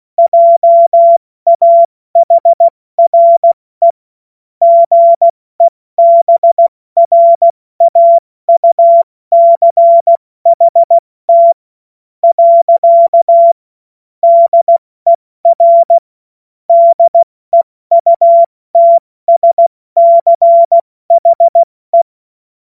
Morse code, heard as a keyed tone, about 16 wpm, spaced as for 14 wpm, 680 Hz. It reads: JAHRE GEBRAUCHT. DER DEUTSCHE
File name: Morsekurs / Witze / Witz 07.November 2023 70 F80